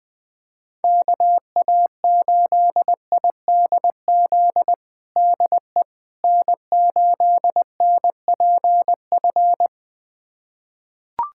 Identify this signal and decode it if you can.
Morse code, heard as a keyed tone, about 20 wpm, 700 Hz. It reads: KA8IDZ DE N8NPF
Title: USA Callsign DE Callsign 20wpm